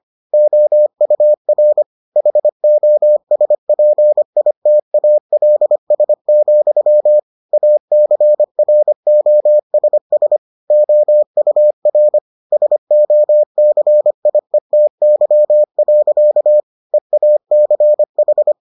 Morse code, heard as a keyed tone, about 25 wpm, 600 Hz. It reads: OUR HOSPITALS, ACROSS OUR SOCIETY. EACH